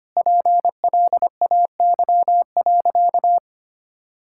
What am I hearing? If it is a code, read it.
Morse code, heard as a keyed tone, about 25 wpm, 700 Hz. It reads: PLAY.
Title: Sentences from Top 1000 Words - Review Part 5 - 25wpm